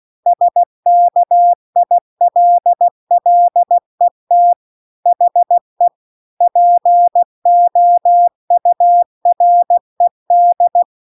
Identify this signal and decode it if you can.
Morse code, heard as a keyed tone, about 16 wpm, 700 Hz. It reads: SKILLET HE POURED